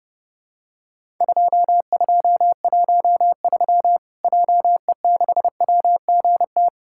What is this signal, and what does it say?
Morse code, heard as a keyed tone, about 30 wpm, 700 Hz. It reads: 2213 JE6WGT
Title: Worked All Europe DX Contest - QTC Exchange - 30wpm